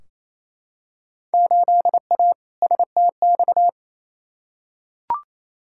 Morse code, heard as a keyed tone, about 28 wpm, 700 Hz. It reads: 8A STX
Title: Field Day Exchanges Encore - 28wpm